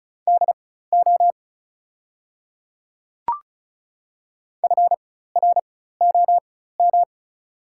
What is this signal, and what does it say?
Morse code, heard as a keyed tone, about 35 wpm, spaced as for 16 wpm, 700 Hz. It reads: DO FROM
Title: 4x Character Spacing - Top 300 Words - 35wpm